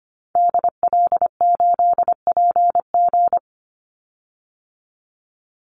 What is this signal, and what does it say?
Morse code, heard as a keyed tone, about 25 wpm, 700 Hz. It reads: DL8PG
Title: Call Signs - 25wpm